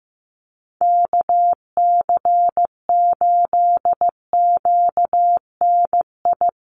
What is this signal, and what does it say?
Morse code, heard as a keyed tone, about 15 wpm, 700 Hz. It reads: KC8QNI
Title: USA Call Signs Repeated 2x 15wpm